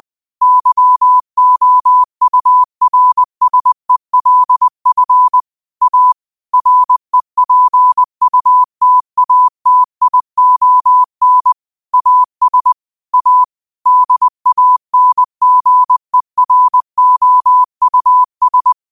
Morse code, heard as a keyed tone, about 20 wpm, 1000 Hz. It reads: YOURSELF A REPUTATION AS A DANGEROUS
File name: qotdfiles / 20wpm